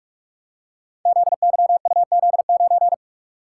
Morse code, heard as a keyed tone, about 44 wpm, 700 Hz. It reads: ZYUZ9
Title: Code Groups - 5 Letter:Numbers - 45wpm